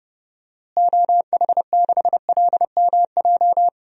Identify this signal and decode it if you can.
Morse code, heard as a keyed tone, about 30 wpm, 700 Hz. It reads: OH6LMJ
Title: DXCC-100 Call Signs - 6 Characters - Repeated 3x - 30wpm